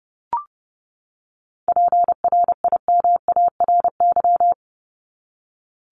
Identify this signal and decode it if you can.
Morse code, heard as a keyed tone, about 30 wpm, 700 Hz. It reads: PRIMARY
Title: LY Suffix - Top 100 Words with Base Words 30wpm